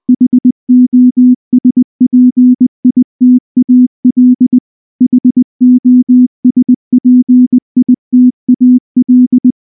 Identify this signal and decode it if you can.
Morse code, heard as a keyed tone, about 19 wpm, 250 Hz. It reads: HOSPITAL HOSPITAL
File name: morse / morse-training-20WPM